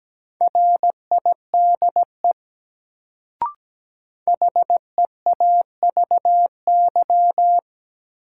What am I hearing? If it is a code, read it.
Morse code, heard as a keyed tone, about 17 wpm, 700 Hz. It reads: RIDE HEAVY
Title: Sentences from Top 1000 Words - Review Part 6 - New Word Drill - 17wpm